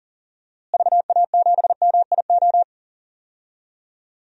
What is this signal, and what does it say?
Morse code, heard as a keyed tone, about 40 wpm, 700 Hz. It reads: VA7MIO
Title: DXCC-100 Call Signs - 6 Characters - Repeated 3x - 40wpm